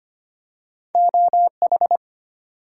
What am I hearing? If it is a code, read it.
Morse code, heard as a keyed tone, about 25 wpm, 700 Hz. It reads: OH
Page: morse-code-ninja-practice-25wpm